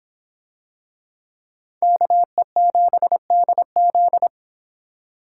Morse code, heard as a keyed tone, about 26 wpm, 700 Hz. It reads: KE7DZ